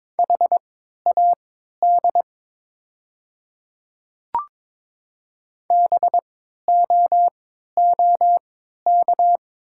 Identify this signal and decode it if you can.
Morse code, heard as a keyed tone, about 22 wpm, spaced as for 12 wpm, 700 Hz. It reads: HAD BOOK
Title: 3x Character Spacing - Top 300 Words - 22wpm